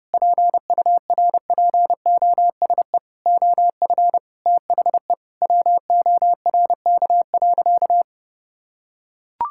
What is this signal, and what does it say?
Morse code, heard as a keyed tone, about 30 wpm, 700 Hz. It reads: PURPOSE OF THE WORK.
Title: Sentences from Top 1000 Words - Review Part 6 - 30wpm